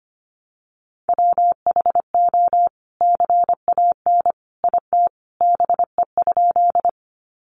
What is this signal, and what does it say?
Morse code, heard as a keyed tone, about 25 wpm, 700 Hz. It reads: WHO CAN IT BE?
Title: Sentences from Top 100 Words 25wpm